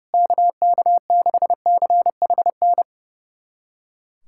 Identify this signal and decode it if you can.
Morse code, heard as a keyed tone, about 30 wpm, 700 Hz. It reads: KK6CHN
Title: USA Callsign DE Callsign 30wpm